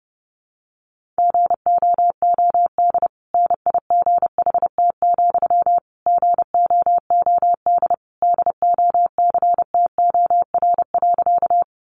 Morse code, heard as a keyed tone, about 30 wpm, 700 Hz. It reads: GOOD NIGHT, GOOD DOCTOR.
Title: Sentences from Top 1500 Words 30wpm